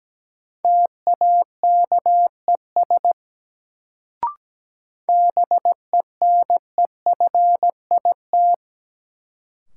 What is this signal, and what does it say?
Morse code, heard as a keyed tone, about 17 wpm, 700 Hz. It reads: TAKES BENEFIT
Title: Sentences from Top 1000 Words - Review Part 6 - New Word Drill - 17wpm